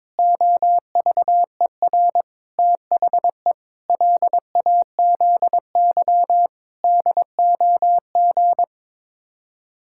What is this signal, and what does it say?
Morse code, heard as a keyed tone, about 22 wpm, 700 Hz. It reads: OVER THE LAZY DOG
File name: practice / 22wpm / Sets of 8 Words - 22wpm